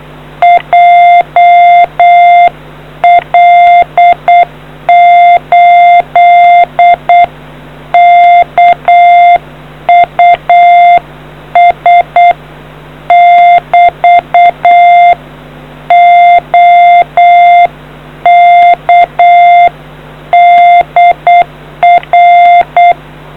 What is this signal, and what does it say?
Morse code, heard as a keyed tone, about 8 wpm, 2100 Hz. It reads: JL8KUS <BT> OK DR